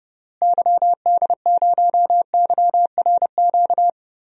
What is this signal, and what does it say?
Morse code, heard as a keyed tone, about 30 wpm, 700 Hz. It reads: YD0YRQ
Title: DXCC-100 Call Signs - 6 Characters - Repeated 3x - 30wpm